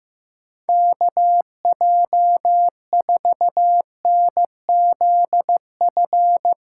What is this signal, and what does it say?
Morse code, heard as a keyed tone, about 15 wpm, 700 Hz. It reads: KJ4NZF
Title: USA Callsign DE Callsign - 6 Characters - 15wpm